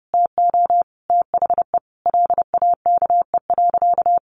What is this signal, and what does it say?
Morse code, heard as a keyed tone, about 30 wpm, 700 Hz. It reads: TO THE LAKE.
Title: Sentences from Top 1500 Words 30wpm